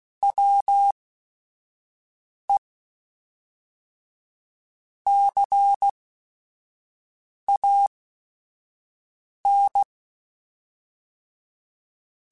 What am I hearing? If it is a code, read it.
Morse code, heard as a keyed tone, about 16 wpm, spaced as for 5 wpm, 800 Hz. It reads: WE CAN
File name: Technical Subjects / Technical Subjects-mp3s / Sw Pwr Sup @5